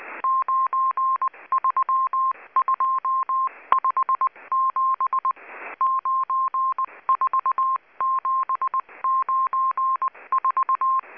Morse code, heard as a keyed tone, about 19 wpm, 1060 Hz. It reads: 93257 94794
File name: HM02- -0540utc